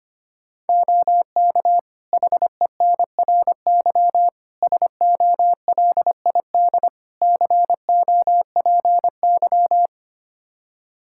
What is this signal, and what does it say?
Morse code, heard as a keyed tone, about 25 wpm, 700 Hz. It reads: OK HENRY SOLID COPY
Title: Practice QSO - Exchange 3 - More Info - 25wpm